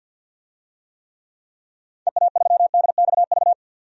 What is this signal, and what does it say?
Morse code, heard as a keyed tone, about 51 wpm, 700 Hz. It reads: EA3DXV
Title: DXCC-100 Call Signs - 6 Characters - 50wpm